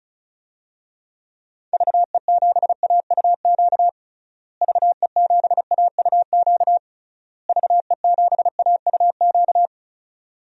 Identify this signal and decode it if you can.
Morse code, heard as a keyed tone, about 36 wpm, 700 Hz. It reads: VE7AUQ VE7AUQ VE7AUQ
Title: DXCC-100 Call Signs - 6 Characters - Repeated 3x - 35wpm